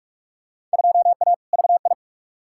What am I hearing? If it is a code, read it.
Morse code, heard as a keyed tone, about 46 wpm, 700 Hz. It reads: 2A VI
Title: Field Day Exchanges - 45wpm